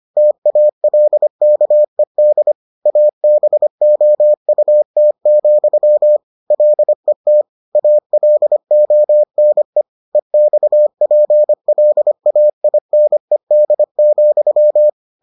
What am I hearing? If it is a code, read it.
Morse code, heard as a keyed tone, about 25 wpm, 600 Hz. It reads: TALKED ABOUT, LET ALONE EXPLAINED,